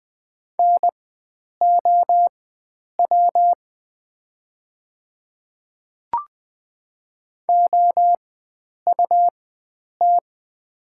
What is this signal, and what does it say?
Morse code, heard as a keyed tone, about 20 wpm, spaced as for 9 wpm, 700 Hz. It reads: NOW OUT
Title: 4x Character Spacing - Top 100 Words - 20wpm